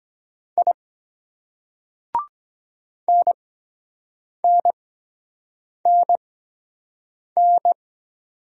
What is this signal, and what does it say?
Morse code, heard as a keyed tone, about 23 wpm, spaced as for 7 wpm, 700 Hz. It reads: I NNNN